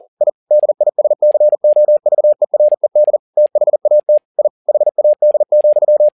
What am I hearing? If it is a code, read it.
Morse code, heard as a keyed tone, about 40 wpm, 600 Hz. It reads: I DISCOVERED THAT I HAD,
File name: learning center / tower trouble / tower touble 40wpm